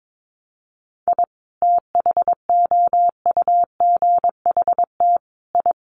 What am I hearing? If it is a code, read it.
Morse code, heard as a keyed tone, about 22 wpm, 700 Hz. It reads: I THOUGHT I
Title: Sets of 5 Words Encore2 22wpm